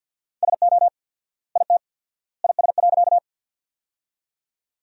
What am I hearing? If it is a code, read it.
Morse code, heard as a keyed tone, about 50 wpm, 700 Hz. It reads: SO IT IS.
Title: 4x Word-Spacing - Sentences from Top 100 Words - 50wpm